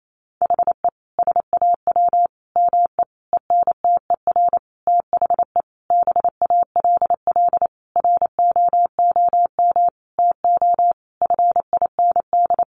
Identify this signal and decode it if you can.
Morse code, heard as a keyed tone, about 28 wpm, 700 Hz. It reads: HE SAW ME ENTER THE BALL ROOM TO FIND
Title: Sentences from Top 1300 Words 28wpm